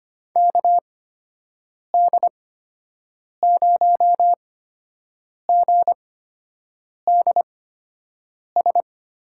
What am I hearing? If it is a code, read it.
Morse code, heard as a keyed tone, about 25 wpm, spaced as for 7 wpm, 700 Hz. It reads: KD0GDS
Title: USA Call Signs - 8x Character Spacing - 25wpm